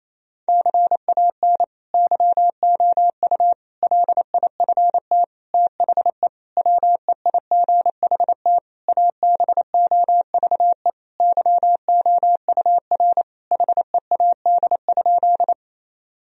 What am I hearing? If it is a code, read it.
Morse code, heard as a keyed tone, about 27 wpm, 700 Hz. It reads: CAN YOU LIFT THE WEIGHT ABOVE YOUR HEAD?
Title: Sentences from Top 1000 Words - Review Part 3 - 28wpm